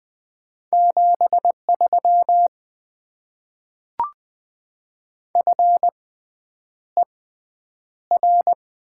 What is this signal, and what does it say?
Morse code, heard as a keyed tone, about 20 wpm, spaced as for 9 wpm, 700 Hz. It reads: 73 FER
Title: Top 20 QSO Elements - 6x Character Spacing - 20wpm